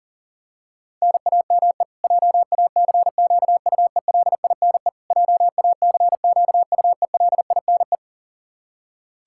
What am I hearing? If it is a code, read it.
Morse code, heard as a keyed tone, about 37 wpm, 700 Hz. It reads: NAME JACQUELINE JACQUELINE
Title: Practice QSO - Exchange 1 - Beginning - 40wpm